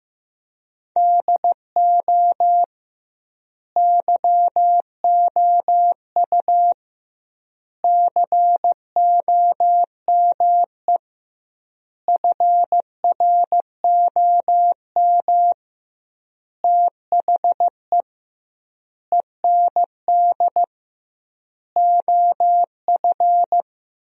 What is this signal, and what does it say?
Morse code, heard as a keyed tone, about 15 wpm, 700 Hz. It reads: DO YOU COME FROM THE END OF
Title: 2x Word-Spacing - Sentences from Top 200 Words - 15wpm